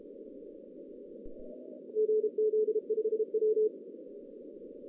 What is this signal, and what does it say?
Morse code, heard as a keyed tone, about 34 wpm, 430 Hz. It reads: GZ5W
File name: 2015 cq ww cw / 2015 cqwww k5tr - 0-18